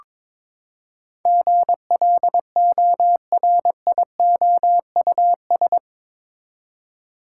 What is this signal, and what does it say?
Morse code, heard as a keyed tone, about 22 wpm, 700 Hz. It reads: GLORIOUS
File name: practice / 22wpm / Eight Letter Words - 22wpm